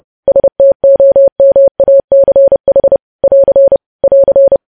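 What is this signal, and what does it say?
Morse code, heard as a keyed tone, about 30 wpm, 550 Hz. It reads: STOMACH <AR> <AR>
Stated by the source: Morse code news headlines practice sent at 30 wpm.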